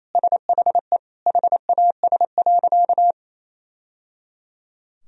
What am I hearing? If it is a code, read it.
Morse code, heard as a keyed tone, about 28 wpm, 700 Hz. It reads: SHE HAS.
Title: Sentences from Top 1000 Words - Review Part 2 - 28wpm